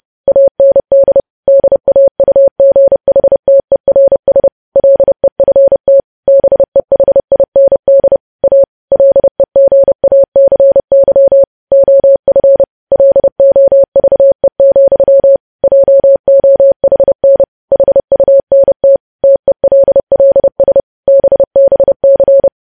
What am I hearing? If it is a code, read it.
Morse code, heard as a keyed tone, about 30 wpm, 550 Hz. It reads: AND DAUGHTERS LEFT BEHIND A LEGACY OF LOVE, JOHN HUNT TELLS BBC
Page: News Headlines in Morse Code at 30 WPM